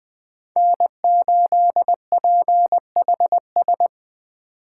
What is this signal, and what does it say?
Morse code, heard as a keyed tone, about 20 wpm, 700 Hz. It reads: N8PHS
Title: Speed Racing - USA 1x3 Call Signs 17wpm